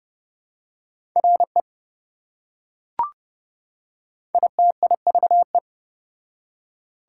Morse code, heard as a keyed tone, about 28 wpm, 700 Hz. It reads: RE ITIVE